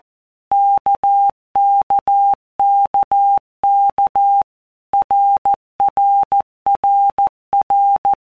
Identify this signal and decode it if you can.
Morse code, heard as a keyed tone, about 14 wpm, 800 Hz. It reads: KKKK RRRR
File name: All Groups 1-minute Blocks / Groups@13 / RKPX